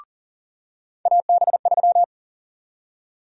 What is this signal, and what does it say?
Morse code, heard as a keyed tone, about 40 wpm, 700 Hz. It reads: AB3
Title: USA Call Sign Prefixes - 40wpm